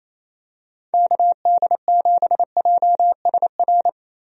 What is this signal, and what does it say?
Morse code, heard as a keyed tone, about 28 wpm, 700 Hz. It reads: KD7JSR
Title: USA Call Signs - 6 Characters - Repeated 3x - 28wpm